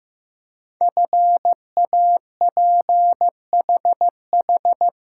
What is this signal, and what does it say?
Morse code, heard as a keyed tone, about 15 wpm, 700 Hz. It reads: FAPHH
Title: Code Groups - 5 Letters - 15wpm